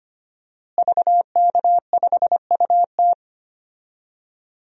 Morse code, heard as a keyed tone, about 25 wpm, 700 Hz. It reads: VK5UT
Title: DXCC-100 Call Signs - 5 Characters - Repeated 2x - 25wpm